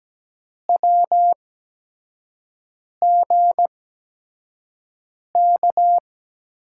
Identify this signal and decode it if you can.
Morse code, heard as a keyed tone, about 17 wpm, spaced as for 5 wpm, 700 Hz. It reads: WGK
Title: USA Call Signs - 6 Characters - 8x Character Spacing - 17wpm